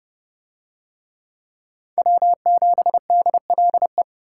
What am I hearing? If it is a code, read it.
Morse code, heard as a keyed tone, about 30 wpm, 700 Hz. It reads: W7DLE